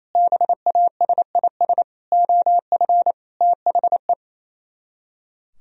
Morse code, heard as a keyed tone, about 36 wpm, 700 Hz. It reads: BASIS OF THE